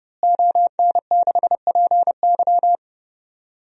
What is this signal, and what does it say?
Morse code, heard as a keyed tone, about 30 wpm, 700 Hz. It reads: ON6PY
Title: DXCC-100 Call Signs - 5 Characters - Repeated 2x - 30wpm